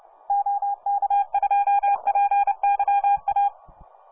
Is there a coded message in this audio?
Yes: Morse code from 0.3 to 3.5 s, heard as a keyed tone, about 29 wpm, 810 Hz. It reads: OK2PYA